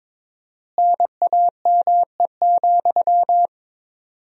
Morse code, heard as a keyed tone, about 22 wpm, 700 Hz. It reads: NAME,
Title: 4x Word-Spacing - Sentences from Top 500 Words - 22wpm